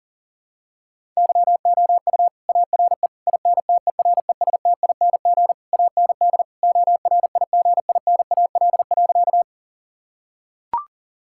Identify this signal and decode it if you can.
Morse code, heard as a keyed tone, about 40 wpm, 700 Hz. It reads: YOU ARE INTERESTING AND ORIGINAL.
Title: Sentences from Top 1000 Words - Review Part 6 - 40wpm